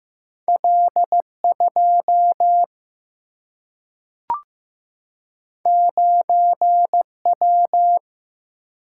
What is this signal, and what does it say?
Morse code, heard as a keyed tone, about 15 wpm, 700 Hz. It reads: L2 9W
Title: Single-Letter Single-Number 15wpm